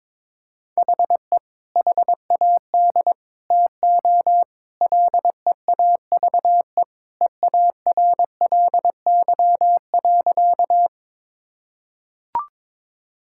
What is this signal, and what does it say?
Morse code, heard as a keyed tone, about 22 wpm, 700 Hz. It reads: HE HAD TO LEAVE EARLY.
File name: practice / 22wpm / Sentences with 2-Word Phrases - 22wpm